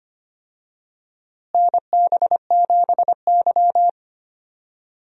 Morse code, heard as a keyed tone, about 25 wpm, 700 Hz. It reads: NB7Y
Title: USA Call Signs - 4 Characters - Repeated 2x - 25wpm